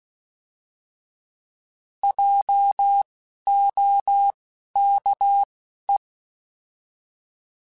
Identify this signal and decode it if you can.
Morse code, heard as a keyed tone, about 16 wpm, spaced as for 11 wpm, 800 Hz. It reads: JOKE
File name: Alphabetical / Alphabetical mp3s / Js @10